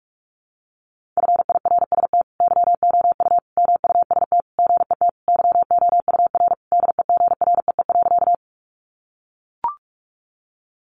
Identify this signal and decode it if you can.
Morse code, heard as a keyed tone, about 44 wpm, 700 Hz. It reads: FIRST YOU MUST GET YOUR DEGREE.
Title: Sentences from Top 1500 Words 45wpm